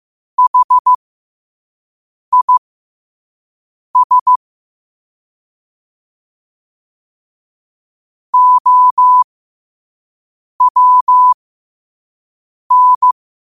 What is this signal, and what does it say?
Morse code, heard as a keyed tone, about 15 wpm, spaced as for 5 wpm, 1000 Hz. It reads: HIS OWN
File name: qotdfiles / 5wpm